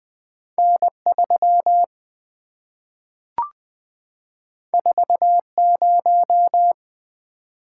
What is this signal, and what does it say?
Morse code, heard as a keyed tone, about 20 wpm, 700 Hz. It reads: N3 40
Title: Two Letter-Number Combinations - 20wpm